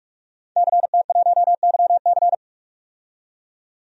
Morse code, heard as a keyed tone, about 43 wpm, 700 Hz. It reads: CT1YC
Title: DXCC-100 Call Signs - 5 Characters - 45wpm